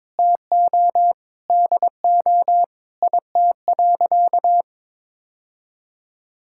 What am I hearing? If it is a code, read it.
Morse code, heard as a keyed tone, about 22 wpm, 700 Hz. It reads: TO DO IT.
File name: practice / 22wpm / Sentences with 3-Word Phrases - 22wpm